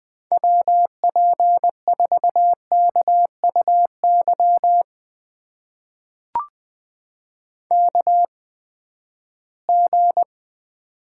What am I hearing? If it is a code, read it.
Morse code, heard as a keyed tone, about 20 wpm, spaced as for 14 wpm, 700 Hz. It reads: WP4KUY KG
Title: USA Call Signs - 6 Characters - 8x Character Spacing - 20wpm